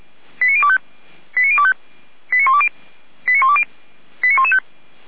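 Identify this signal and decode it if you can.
Five-tone selective call (ZVEI), heard as quick runs of five tones, tones of about 70 ms. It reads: 89025 89025 89120 89120 89115
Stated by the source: ZVEI five-tone selective call received via SDR.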